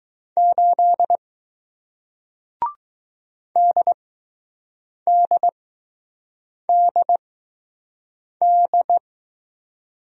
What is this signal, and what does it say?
Morse code, heard as a keyed tone, about 20 wpm, spaced as for 6 wpm, 700 Hz. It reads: 8 DDDD